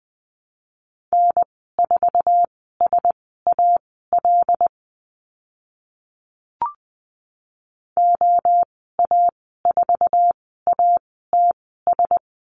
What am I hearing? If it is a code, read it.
Morse code, heard as a keyed tone, about 20 wpm, spaced as for 14 wpm, 700 Hz. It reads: N4SAL OA4ATS
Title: 076 Lesson 13g - Letter L - Call Signs 20wpm